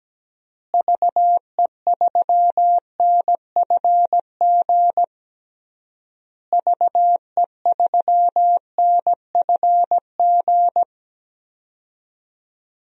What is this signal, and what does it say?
Morse code, heard as a keyed tone, about 17 wpm, 700 Hz. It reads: VE3NFG VE3NFG
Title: DXCC-100 Call Signs - 6 Characters - Repeated 3x - 17wpm